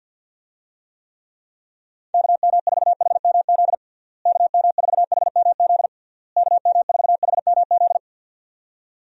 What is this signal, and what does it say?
Morse code, heard as a keyed tone, about 51 wpm, 700 Hz. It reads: KM4HMZ KM4HMZ KM4HMZ
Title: USA Call Signs - 6 Characters - Repeated 3x - 50wpm